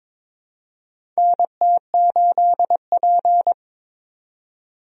Morse code, heard as a keyed tone, about 22 wpm, 700 Hz. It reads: NT8P
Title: USA Call Signs - 4 Characters - 22wpm